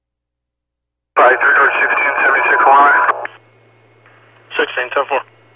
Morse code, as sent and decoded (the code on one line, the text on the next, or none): none